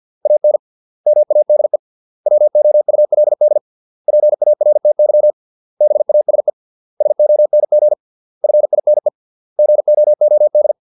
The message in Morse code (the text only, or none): AN MADE WOULD PARTY BASE SONG FINE GOOD